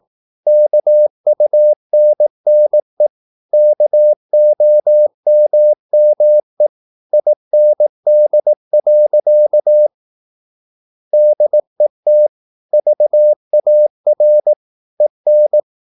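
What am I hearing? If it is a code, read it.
Morse code, heard as a keyed tone, about 18 wpm, 600 Hz. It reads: KUNNE KOMME IND. DET VAR EN
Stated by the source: Slow (QRS) Morse practice at 18 wpm, CW med Gnister.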